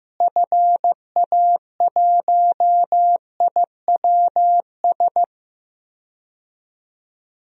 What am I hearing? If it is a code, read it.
Morse code, heard as a keyed tone, about 15 wpm, 700 Hz. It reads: FA1IWS
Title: DXCC-100 Call Signs - 6 Characters - Repeated 3x - 15wpm